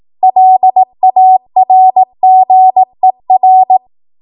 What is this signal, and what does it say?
Morse code, heard as a keyed tone, about 18 wpm, 750 Hz. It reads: LARGER
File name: Morse / 18WPM